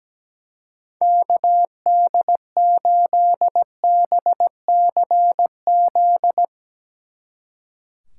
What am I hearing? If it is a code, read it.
Morse code, heard as a keyed tone, about 17 wpm, 700 Hz. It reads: KD8BCZ
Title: Speed Racing - USA Call Signs 17wpm